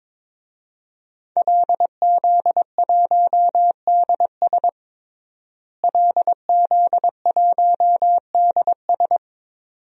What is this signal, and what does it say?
Morse code, heard as a keyed tone, about 22 wpm, 700 Hz. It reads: LZ1DS LZ1DS
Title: DXCC-100 Call Signs - 5 Characters - Repeated 2x - 22wpm